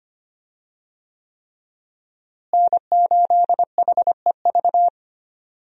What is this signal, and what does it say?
Morse code, heard as a keyed tone, about 25 wpm, 700 Hz. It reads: N8HEV
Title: USA Call Signs - 5 Characters - Repeated 3x - 25wpm